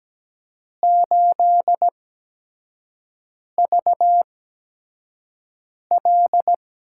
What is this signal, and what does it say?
Morse code, heard as a keyed tone, about 17 wpm, spaced as for 5 wpm, 700 Hz. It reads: 8VL
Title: USA Call Signs - 6 Characters - 8x Character Spacing - 17wpm